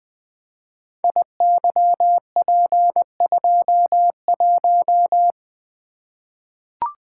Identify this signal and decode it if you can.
Morse code, heard as a keyed tone, about 20 wpm, 700 Hz. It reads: IYP21
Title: Code Groups - 5 Letter:Numbers - 20wpm